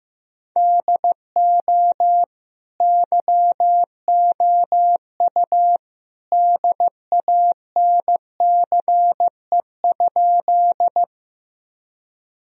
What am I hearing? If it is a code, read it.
Morse code, heard as a keyed tone, about 15 wpm, 700 Hz. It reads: DO YOU DANCE?
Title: Sentences from Top 1000 Words - Review Part 4 - 15wpm